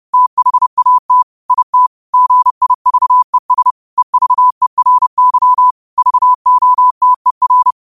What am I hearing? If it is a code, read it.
Morse code, heard as a keyed tone, about 30 wpm, 1000 Hz. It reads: THAT IT GIVES EVERY VOTER
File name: qotdfiles / 30wpm